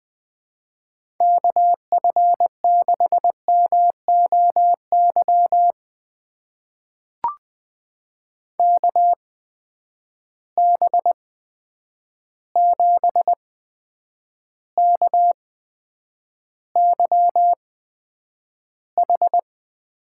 Morse code, heard as a keyed tone, about 20 wpm, spaced as for 9 wpm, 700 Hz. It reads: KF6MOY KB7KYH
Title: USA Call Signs - 6 Characters - 8x Character Spacing - 20wpm